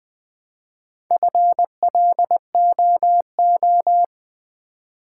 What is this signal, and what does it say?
Morse code, heard as a keyed tone, about 20 wpm, 700 Hz. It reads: FLOO
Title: Build A Word - 1 Letter At A Time - 5-Letter Words - 20wpm